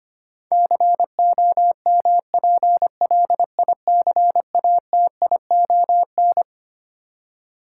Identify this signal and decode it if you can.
Morse code, heard as a keyed tone, about 25 wpm, 700 Hz. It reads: COMPLICATION
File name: Twelve Letter Words - 25wpm